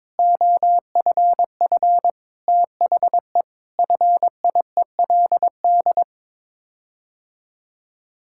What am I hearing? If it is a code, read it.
Morse code, heard as a keyed tone, about 22 wpm, 700 Hz. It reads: OFF THE FIELD
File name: practice / 22wpm / Sets of 6 Words - 22wpm